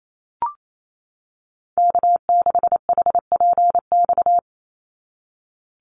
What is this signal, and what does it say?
Morse code, heard as a keyed tone, about 28 wpm, 700 Hz. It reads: K6HPX
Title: NA SOTA Activator Callsigns 28wpm